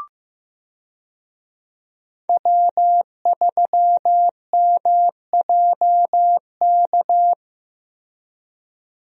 Morse code, heard as a keyed tone, about 15 wpm, 700 Hz. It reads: W3MJK
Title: USA Call Signs - 5 Characters - Repeated 2x - 15wpm